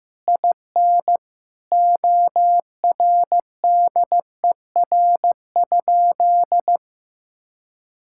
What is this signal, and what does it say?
Morse code, heard as a keyed tone, about 15 wpm, 700 Hz. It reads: IN ORDER?
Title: Sentences from Top 1000 Words - Review Part 6 - 15wpm